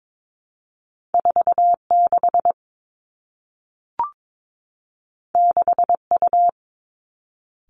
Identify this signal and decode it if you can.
Morse code, heard as a keyed tone, about 22 wpm, 700 Hz. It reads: 4 6 6 U
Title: Two Challenging Letter-Number Combinations 22wpm